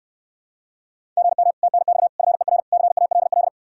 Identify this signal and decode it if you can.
Morse code, heard as a keyed tone, about 69 wpm, 700 Hz. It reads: GM TNX FER CALL